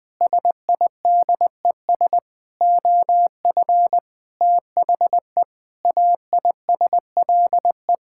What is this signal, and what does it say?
Morse code, heard as a keyed tone, about 20 wpm, 700 Hz. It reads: SIDES OF THE AISLE
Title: Sentences with 6-Word Phrases - 20wpm